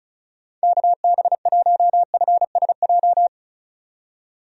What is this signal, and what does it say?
Morse code, heard as a keyed tone, about 36 wpm, 700 Hz. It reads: KB1FSJ
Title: USA Call Signs - 6 Characters - 35wpm